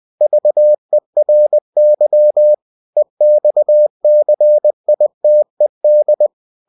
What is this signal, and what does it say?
Morse code, heard as a keyed tone, about 20 wpm, 600 Hz. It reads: VERY EXCITED
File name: cw-books / winnie-the-pooh / winnie-20wpm-page-0093